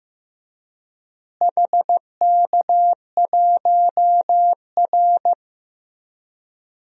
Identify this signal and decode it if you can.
Morse code, heard as a keyed tone, about 15 wpm, 700 Hz. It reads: HK1R
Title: DXCC-100 Call Signs - 4 Characters - Repeated 3x - 15wpm